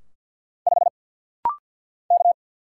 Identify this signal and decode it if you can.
Morse code, heard as a keyed tone, about 48 wpm, 700 Hz. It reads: 5 K
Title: Single Letter-Number - Rapid-Fire 50wpm